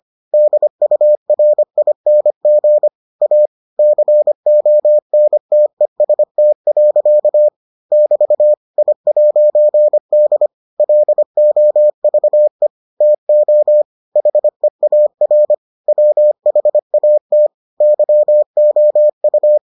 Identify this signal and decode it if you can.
Morse code, heard as a keyed tone, about 25 wpm, 600 Hz. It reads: DURING A CONTEST. <BT> I'D LOVE TO HEAR WHAT YOU